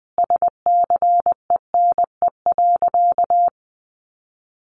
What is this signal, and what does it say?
Morse code, heard as a keyed tone, about 20 wpm, 700 Hz. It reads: SCENE.
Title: Sentences from Top 1100 Words 20wpm